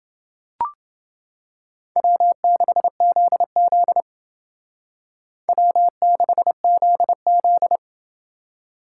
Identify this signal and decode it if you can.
Morse code, heard as a keyed tone, about 31 wpm, 700 Hz. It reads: W6ZZ W6ZZ